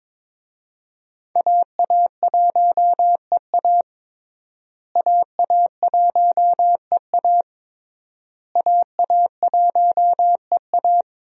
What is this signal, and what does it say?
Morse code, heard as a keyed tone, about 22 wpm, 700 Hz. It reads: AA1EA AA1EA AA1EA
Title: USA Call Signs - 5 Characters - Repeated 3x - 22wpm